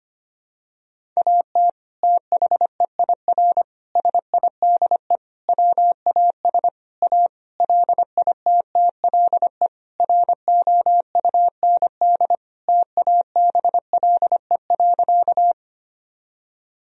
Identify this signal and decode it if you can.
Morse code, heard as a keyed tone, about 25 wpm, 700 Hz. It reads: AT THEIR SIDE WAS A LITTLE ROUND TABLE.
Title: Sentences from Top 1000 Words - Review Part 2 - 25wpm